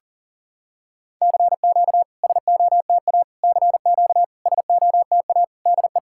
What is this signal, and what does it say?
Morse code, heard as a keyed tone, about 44 wpm, 700 Hz. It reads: CQ SOTA CQ SOTA DE
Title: SOTA QSO Practice 40wpm